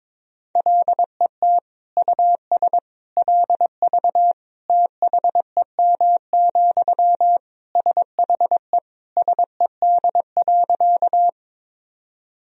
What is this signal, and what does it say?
Morse code, heard as a keyed tone, about 22 wpm, 700 Hz. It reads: LET US LV THEM, SHE SED.
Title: Sentences from Top 400 Words and Abbreviations 22wpm